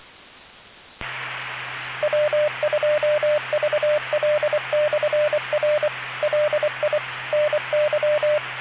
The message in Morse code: W2VL/R LI NY